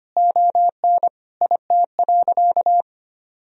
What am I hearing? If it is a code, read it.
Morse code, heard as a keyed tone, about 25 wpm, 700 Hz. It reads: ON IT.
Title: Sentences from Top 1000 Words - Review Part 3 - 25wpm